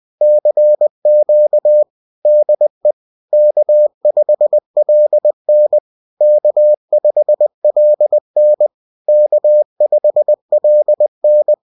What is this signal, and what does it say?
Morse code, heard as a keyed tone, about 20 wpm, 600 Hz. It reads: CQ DE K5LN K5LN K5LN